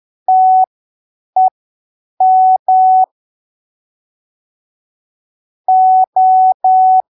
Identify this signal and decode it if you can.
Morse code, heard as a keyed tone, about 10 wpm, spaced as for 7 wpm, 750 Hz. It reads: TEM O